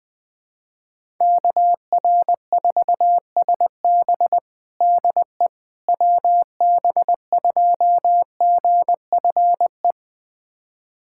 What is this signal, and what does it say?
Morse code, heard as a keyed tone, about 21 wpm, 700 Hz. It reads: KR4SB DE WB2GFE
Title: USA Callsign DE Callsign 20wpm